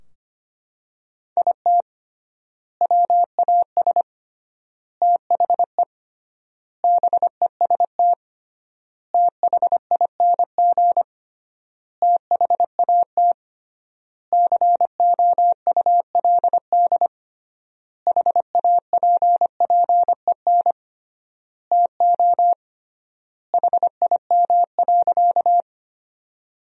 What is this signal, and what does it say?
Morse code, heard as a keyed tone, about 25 wpm, 700 Hz. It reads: IT WAS THE BEST THING THAT COULD HAPPEN TO HIM.
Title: 3x Word-Spacing - Sentences from Top 400 Words - 25wpm